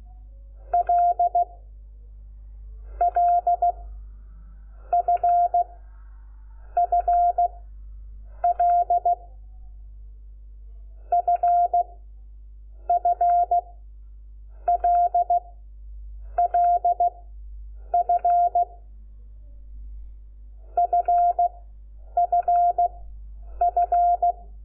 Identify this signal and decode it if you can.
Morse code, heard as a keyed tone, about 16 wpm, 690 Hz. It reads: L L F F L F F L L F F F F
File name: Lecciones / CLASE Nº5